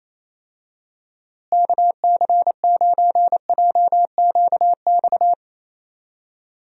Morse code, heard as a keyed tone, about 28 wpm, 700 Hz. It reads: KC9JQX
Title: USA Call Signs - 6 Characters - Repeated 3x - 28wpm